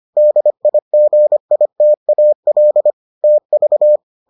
Morse code, heard as a keyed tone, about 25 wpm, 600 Hz. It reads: DIGITAL TV